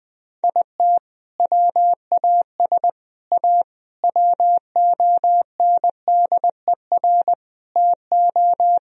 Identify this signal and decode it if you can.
Morse code, heard as a keyed tone, about 21 wpm, 700 Hz. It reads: IT WAS A WONDER TO
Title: Sentences from Top 1000 Words - Review Part 3 - 20wpm